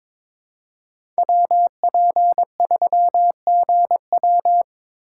Morse code, heard as a keyed tone, about 23 wpm, 700 Hz. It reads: WP3GW
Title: USA Call Signs - 5 Characters - 8x Character Spacing - 22wpm